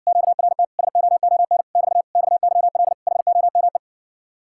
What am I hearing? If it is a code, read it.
Morse code, heard as a keyed tone, about 60 wpm, 700 Hz. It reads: QRT SOON <BT> XYL HOME